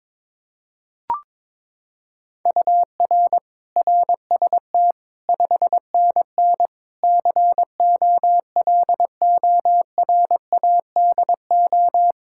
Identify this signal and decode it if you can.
Morse code, heard as a keyed tone, about 22 wpm, 700 Hz. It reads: UR RST 5NN COLORADO
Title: Practice QSO - Exchange 1 - Beginning - 22wpm